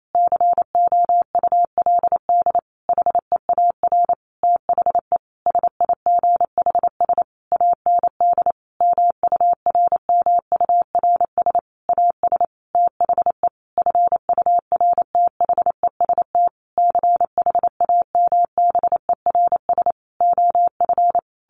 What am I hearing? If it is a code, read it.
Morse code, heard as a keyed tone, about 28 wpm, 700 Hz. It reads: COULD HEAR THE SIGHS AND MURMURS AS THE FURTHEST CHAMBERS OF
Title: A Sound Of Thunder 28wpm